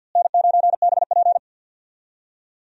Morse code, heard as a keyed tone, about 50 wpm, 700 Hz. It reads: N9BP
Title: USA Call Signs - 4 Characters - Repeated 2x - 50wpm